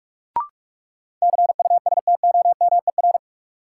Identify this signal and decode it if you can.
Morse code, heard as a keyed tone, about 46 wpm, 700 Hz. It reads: CUSTOMER
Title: Sentences from Top 1100 Words - New Word Drill Alt - 45wpm